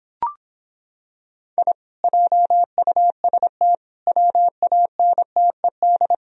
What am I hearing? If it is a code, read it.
Morse code, heard as a keyed tone, about 26 wpm, 700 Hz. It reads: I JUST WANTED